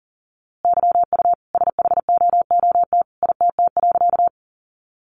Morse code, heard as a keyed tone, about 40 wpm, 700 Hz. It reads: YU SHOOT ITT.
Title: A Sound Of Thunder 40wpm